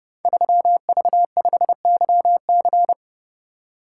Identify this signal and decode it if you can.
Morse code, heard as a keyed tone, about 30 wpm, 700 Hz. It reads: 3V5YC
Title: Code Groups - 5 Letter:Numbers - 30wpm